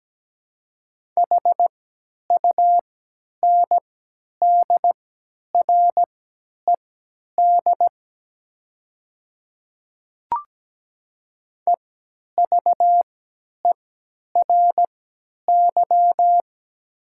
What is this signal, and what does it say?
Morse code, heard as a keyed tone, about 17 wpm, spaced as for 10 wpm, 690 Hz. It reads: HUNDRED EVERY
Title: 3x Character Spacing - Top 500 Words - 17wpm